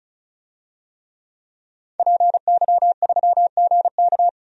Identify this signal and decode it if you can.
Morse code, heard as a keyed tone, about 34 wpm, 700 Hz. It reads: PY3GK
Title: DXCC-100 Call Signs - 5 Characters - 35wpm